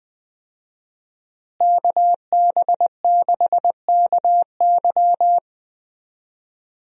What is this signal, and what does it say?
Morse code, heard as a keyed tone, about 21 wpm, 700 Hz. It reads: KB6KY
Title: USA Call Signs - 5 Characters - Repeated 3x - 20wpm